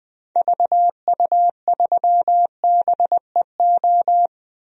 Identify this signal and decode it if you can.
Morse code, heard as a keyed tone, about 20 wpm, 700 Hz. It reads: VU3BEO
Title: DXCC-100 Call Signs - 6 Characters - Repeated 2x - 20wpm